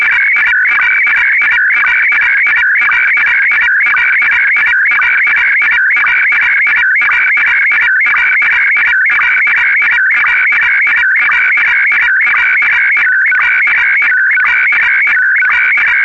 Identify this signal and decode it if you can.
SSTV (slow-scan television), Scottie DX.
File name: sstv / sstv2 / scottie dx